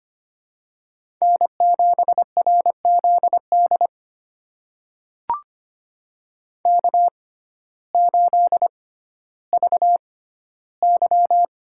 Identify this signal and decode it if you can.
Morse code, heard as a keyed tone, about 25 wpm, spaced as for 14 wpm, 700 Hz. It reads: N7RZD K8VY
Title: USA Call Signs - 6x Character Spacing - 25wpm